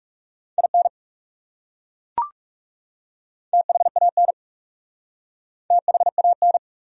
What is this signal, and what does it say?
Morse code, heard as a keyed tone, about 40 wpm, 700 Hz. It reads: IN THAN THAN